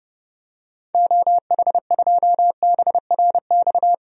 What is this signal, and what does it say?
Morse code, heard as a keyed tone, about 29 wpm, 700 Hz. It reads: OH2BRX
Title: DXCC-100 Call Signs - 6 Characters - 30wpm